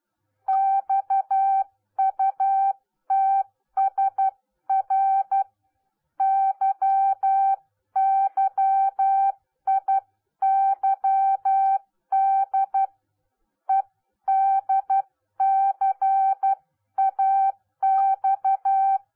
Morse code, heard as a keyed tone, about 12 wpm, 780 Hz. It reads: XUTSR YYIYD EDCAX